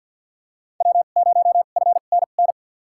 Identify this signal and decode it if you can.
Morse code, heard as a keyed tone, about 47 wpm, spaced as for 35 wpm, 700 Hz. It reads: W0FNN
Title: USA Call Signs - 5 Characters - 2x Character Spacing - 50wpm